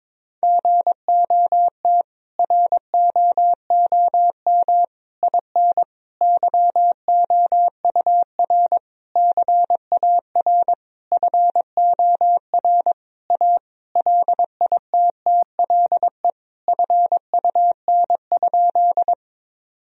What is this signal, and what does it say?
Morse code, heard as a keyed tone, about 22 wpm, 700 Hz. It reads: GOT ROOM IN YOUR CAR FOR A LITTLE FUN?
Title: Sentences from Top 1000 Words - Review Part 1 - 22wpm